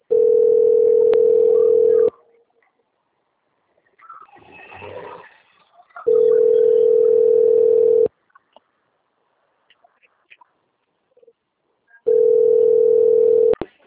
Ringback tone (North America), 440 + 480 Hz.